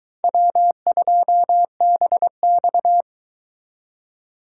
Morse code, heard as a keyed tone, about 23 wpm, 700 Hz. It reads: W2BX